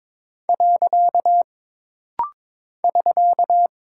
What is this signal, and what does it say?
Morse code, heard as a keyed tone, about 22 wpm, 700 Hz. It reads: . <SK>
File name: practice / 22wpm / Single Punctuation-Prosign - Rapid-Fire - 22wpm